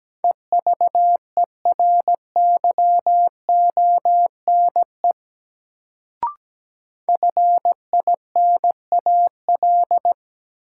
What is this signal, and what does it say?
Morse code, heard as a keyed tone, about 17 wpm, 700 Hz. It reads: EVERYONE FINAL
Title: Sentences from Top 800 Words - New Word Drill - 17wpm